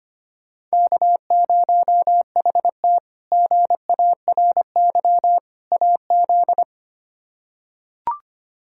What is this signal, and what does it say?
Morse code, heard as a keyed tone, about 25 wpm, 700 Hz. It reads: K0HT GARY AZ
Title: K1USN SST QSO Practice - 25wpm